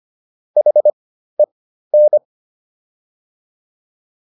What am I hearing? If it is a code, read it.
Morse code, heard as a keyed tone, about 25 wpm, spaced as for 13 wpm, 600 Hz. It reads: HEN